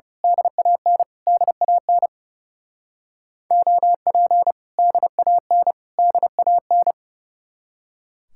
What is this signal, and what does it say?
Morse code, heard as a keyed tone, about 28 wpm, 700 Hz. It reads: DAN DAN OP DAN DAN
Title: Speed-Racing Common QSO Phrases - 30wpm